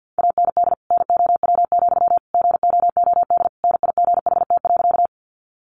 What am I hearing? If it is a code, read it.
Morse code, heard as a keyed tone, about 48 wpm, 700 Hz. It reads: AND NOW, GOOD NIGHT.
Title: Sentences from Top 300 Words 50wpm